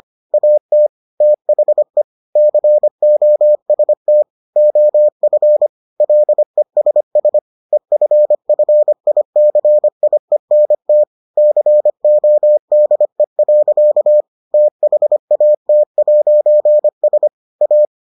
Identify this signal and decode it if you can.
Morse code, heard as a keyed tone, about 25 wpm, 600 Hz. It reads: AT THE COST OF LESS EFFICIENT CODE. THAT'S A